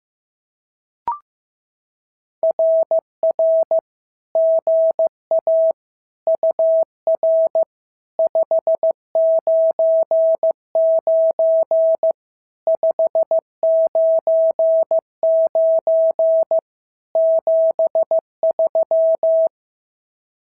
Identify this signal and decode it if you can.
Morse code, heard as a keyed tone, about 15 wpm, 690 Hz. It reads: RR GA UR 599 599 73
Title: SOTA QSO Practice 15wpm